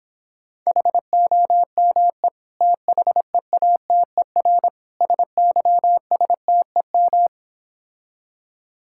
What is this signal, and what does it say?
Morse code, heard as a keyed tone, about 26 wpm, 700 Hz. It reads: HOME THEATER SYSTEM